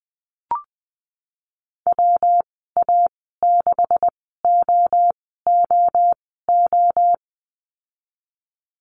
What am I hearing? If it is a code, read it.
Morse code, heard as a keyed tone, about 20 wpm, spaced as for 15 wpm, 700 Hz. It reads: WA6OOO
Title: 173 Lesson 28g - Number 9 - Call Signs 20wpm